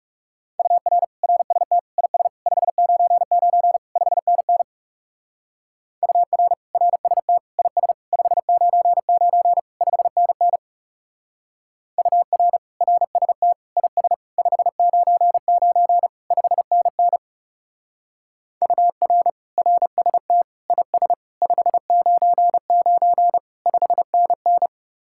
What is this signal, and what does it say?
Morse code, heard as a keyed tone, about 35 wpm, 700 Hz. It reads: UR RST IS 599 5NN UR RST IS 599 5NN UR RST IS 599 5NN UR RST IS 599 5NN
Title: Speed-Racing Common QSO Phrases Encore - 30wpm